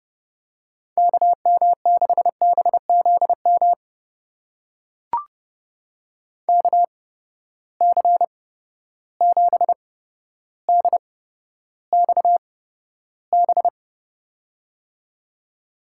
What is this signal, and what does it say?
Morse code, heard as a keyed tone, about 32 wpm, spaced as for 13 wpm, 700 Hz. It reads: KM6BZM KC7DXB
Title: USA Call Signs - 6 Characters - 8x Character Spacing - 30wpm